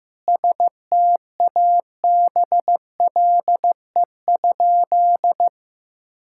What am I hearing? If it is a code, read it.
Morse code, heard as a keyed tone, about 16 wpm, 700 Hz. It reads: STABLE?
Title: Sentences from Top 1600 Words 15wpm